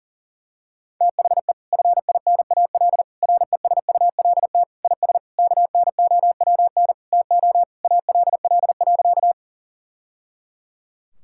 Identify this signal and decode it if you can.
Morse code, heard as a keyed tone, about 40 wpm, 700 Hz. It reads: THE FINAL RESULT IS KNOWN TO ALL.
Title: Sentences from Top 1000 Words - Review Part 5 - 40wpm